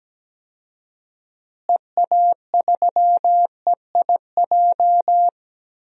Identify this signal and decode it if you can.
Morse code, heard as a keyed tone, about 17 wpm, 700 Hz. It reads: EA3EIJ
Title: DXCC-100 Call Signs - 6 Characters - Repeated 3x - 17wpm